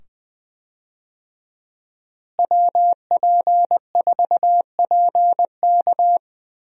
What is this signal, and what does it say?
Morse code, heard as a keyed tone, about 20 wpm, 700 Hz. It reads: WP4PK
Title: USA Call Signs - 5 Characters - Repeated 2x - 20wpm